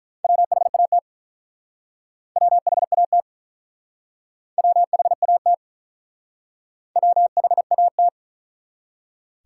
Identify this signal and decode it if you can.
Morse code, heard as a keyed tone, about 44 wpm, 700 Hz. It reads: WHAT WHAT WHAT WHAT